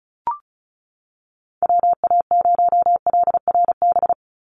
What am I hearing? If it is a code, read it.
Morse code, heard as a keyed tone, about 34 wpm, 700 Hz. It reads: WA0LRB
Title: USA Call Signs 35wpm